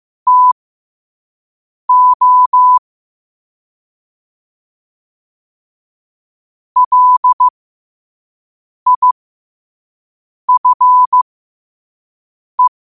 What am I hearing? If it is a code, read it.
Morse code, heard as a keyed tone, about 15 wpm, spaced as for 5 wpm, 1000 Hz. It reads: TO LIFE
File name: qotdfiles / 5wpm